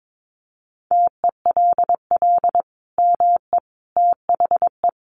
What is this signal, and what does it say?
Morse code, heard as a keyed tone, about 22 wpm, 700 Hz. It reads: TELL ME THE
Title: Sentences from Top 800 Words 22wpm